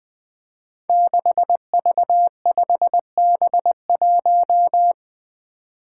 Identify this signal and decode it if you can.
Morse code, heard as a keyed tone, about 21 wpm, 700 Hz. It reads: 6V5B1
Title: Five Challenging Letter-Number Combinations 20wpm